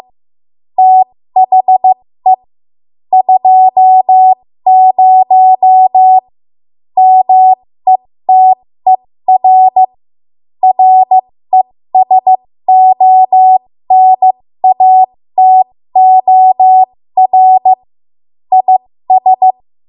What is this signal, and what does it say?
Morse code, heard as a keyed tone, about 15 wpm, spaced as for 13 wpm, 750 Hz. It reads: THE 20 METER RESONATOR IS
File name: Morse / 13WPM